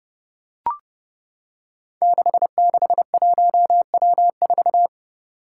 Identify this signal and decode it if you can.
Morse code, heard as a keyed tone, about 30 wpm, 700 Hz. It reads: 661W4
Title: Five Challenging Letter-Number Combinations 30wpm